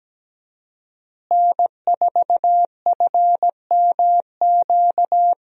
Morse code, heard as a keyed tone, about 17 wpm, 700 Hz. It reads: N4FMQ
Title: USA Call Signs - 5 Characters - Repeated 3x - 17wpm